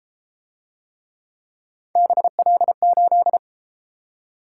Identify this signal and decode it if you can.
Morse code, heard as a keyed tone, about 33 wpm, 700 Hz. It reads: BL8